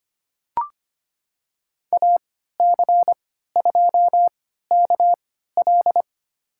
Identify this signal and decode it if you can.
Morse code, heard as a keyed tone, about 25 wpm, spaced as for 14 wpm, 700 Hz. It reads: AC2KL
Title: USA Call Signs - 5 Characters - 3x Character Spacing - 25wpm